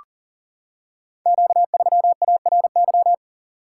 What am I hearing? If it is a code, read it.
Morse code, heard as a keyed tone, about 42 wpm, 700 Hz. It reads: Q3ARY
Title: Code Groups - 5 Letter:Numbers - 40wpm